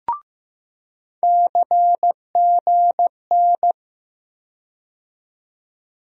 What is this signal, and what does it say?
Morse code, heard as a keyed tone, about 15 wpm, 700 Hz. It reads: CGN
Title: USA Call Sign Postfixes - 3 Letters 15wpm